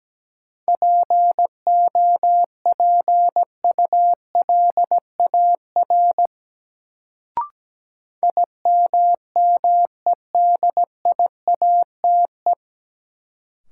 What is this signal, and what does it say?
Morse code, heard as a keyed tone, about 17 wpm, 700 Hz. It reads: POPULAR IMMEDIATE
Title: Sentences from Top 1200 Words - New Word Drill Alt - 17wpm